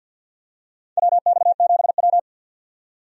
Morse code, heard as a keyed tone, about 53 wpm, 700 Hz. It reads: WX7W
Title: USA Call Signs - 4 Characters - 50wpm